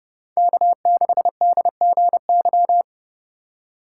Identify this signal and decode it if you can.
Morse code, heard as a keyed tone, about 29 wpm, 700 Hz. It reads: K6DGY
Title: USA Call Signs - 5 Characters - 4x Character Spacing - 30wpm